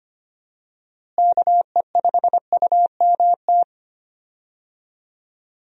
Morse code, heard as a keyed tone, about 25 wpm, 700 Hz. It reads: KE5UMT
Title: USA Call Signs - 6 Characters - Repeated 2x - 25wpm